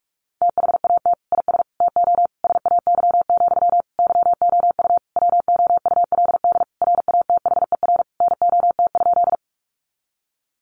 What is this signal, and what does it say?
Morse code, heard as a keyed tone, about 44 wpm, 700 Hz. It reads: THAT IS TO SAY, YOU WOULD RATHER NOT?
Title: Sentences from Top 700 Words 45wpm